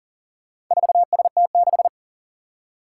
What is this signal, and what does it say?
Morse code, heard as a keyed tone, about 40 wpm, 700 Hz. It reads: 4ST6
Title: Four Challenging Letter-Number Combinations 40wpm